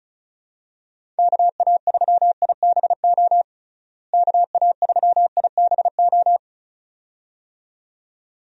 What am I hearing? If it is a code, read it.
Morse code, heard as a keyed tone, about 36 wpm, 700 Hz. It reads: KA3IBO KA3IBO
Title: USA Call Signs - 6 Characters - Repeated 2x - 35wpm